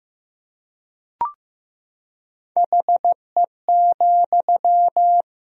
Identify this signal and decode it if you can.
Morse code, heard as a keyed tone, about 15 wpm, 700 Hz. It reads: HE,
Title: Sentences from Top 1000 Words - Review Part 3 - 15wpm